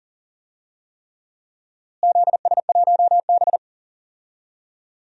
Morse code, heard as a keyed tone, about 40 wpm, 700 Hz. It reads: ZS1B
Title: DXCC-100 Call Signs - 4 Characters - Repeated 3x - 40wpm